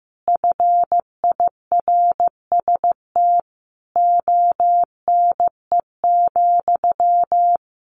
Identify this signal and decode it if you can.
Morse code, heard as a keyed tone, about 15 wpm, 700 Hz. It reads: FIRST ONE,
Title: A Sound Of Thunder 15wpm